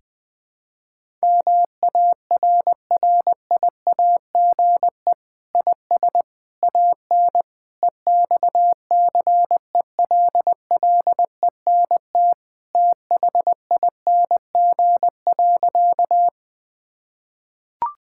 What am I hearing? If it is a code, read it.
Morse code, heard as a keyed tone, about 20 wpm, 700 Hz. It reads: MARRIAGE IS AN EXCELLENT THING.
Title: Sentences from Top 1000 Words - Review Part 5 - 20wpm